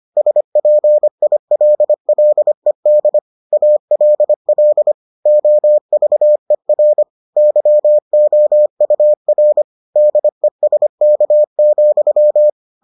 Morse code, heard as a keyed tone, about 25 wpm, 600 Hz. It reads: SPILLED ALL OVER YOUR DESK,